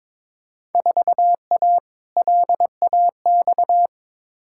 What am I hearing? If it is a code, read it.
Morse code, heard as a keyed tone, about 22 wpm, 700 Hz. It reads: 4A LAX
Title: Field Day Exchanges - 22wpm